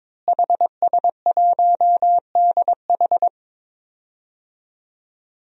Morse code, heard as a keyed tone, about 22 wpm, 700 Hz. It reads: HS1DH
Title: DXCC-100 Call Signs - 5 Characters - Repeated 2x - 22wpm